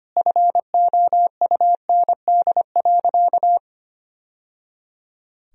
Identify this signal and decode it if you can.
Morse code, heard as a keyed tone, about 25 wpm, 700 Hz. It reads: FOUND.
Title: Sentences from Top 1000 Words - Review Part 4 - 25wpm